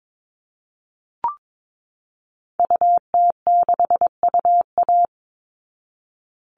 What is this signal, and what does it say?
Morse code, heard as a keyed tone, about 22 wpm, 700 Hz. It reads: UT6UA
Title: Call Signs Repeated 3x - 22wpm